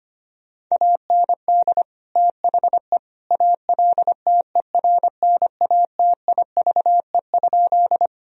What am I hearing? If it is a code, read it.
Morse code, heard as a keyed tone, about 25 wpm, 700 Hz. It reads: AND THE ALTERNATIVE?
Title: Sentences from Top 1000 Words - Review Part 4 - 25wpm